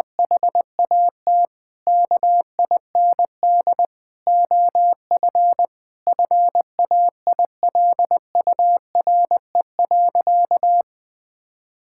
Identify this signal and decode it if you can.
Morse code, heard as a keyed tone, about 20 wpm, 700 Hz. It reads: HAT KIND OF FAILURE.
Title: Sentences from Top 1000 Words - Review Part 6 - 20wpm